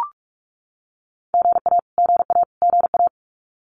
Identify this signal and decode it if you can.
Morse code, heard as a keyed tone, about 44 wpm, 700 Hz. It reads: GA GA GA
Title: Abbreviations Repeated 3x - 45wpm